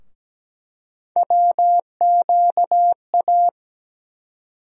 Morse code, heard as a keyed tone, about 17 wpm, 700 Hz. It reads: WQA
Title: USA Call Sign Postfixes - 3 Letters 17wpm